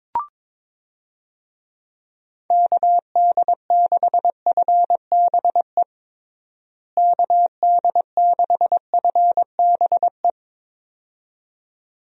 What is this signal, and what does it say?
Morse code, heard as a keyed tone, about 22 wpm, 700 Hz. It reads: KD6FBE KD6FBE
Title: USA Call Signs - 6 Characters - Repeated 2x - 22wpm